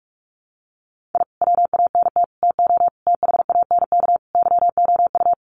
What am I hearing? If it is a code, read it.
Morse code, heard as a keyed tone, about 44 wpm, 700 Hz. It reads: I WANT TO THANK YOU
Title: Sets of 5 Words Encore 45wpm